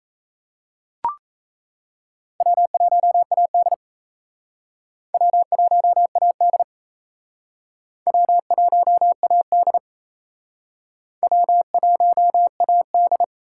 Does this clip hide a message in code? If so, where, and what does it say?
Yes: Morse code from 2.4 to 13.2 s, heard as a keyed tone, about 36 wpm, 700 Hz. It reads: W1AD W1AD W1AD W1AD